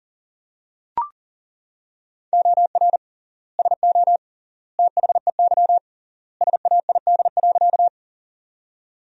Morse code, heard as a keyed tone, about 40 wpm, 700 Hz. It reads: OR SO THEY SAID.
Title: 3x Word-Spacing - Sentences from Top 100 Words - 40wpm